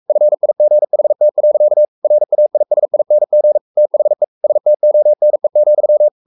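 Morse code, heard as a keyed tone, about 44 wpm, 600 Hz. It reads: FIGHT. RAISING THE STONE,
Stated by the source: Morse code at 43 wpm sending Ken Follett's novel Never in English, from CW med Gnister.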